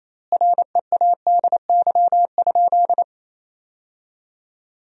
Morse code, heard as a keyed tone, about 28 wpm, 700 Hz. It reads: READY?
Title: 4x Word-Spacing - Sentences from Top 400 Words - 28wpm